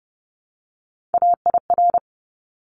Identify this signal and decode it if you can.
Morse code, heard as a keyed tone, about 30 wpm, 700 Hz. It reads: AIR
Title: Top 200 Words Repeated 30wpm